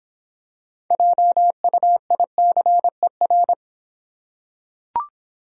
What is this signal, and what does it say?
Morse code, heard as a keyed tone, about 26 wpm, 700 Hz. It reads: JUICER